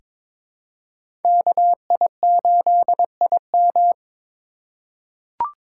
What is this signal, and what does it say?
Morse code, heard as a keyed tone, about 22 wpm, 700 Hz. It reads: KI8IM
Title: USA Call Signs - 5 Characters - 2x Character Spacing - 22wpm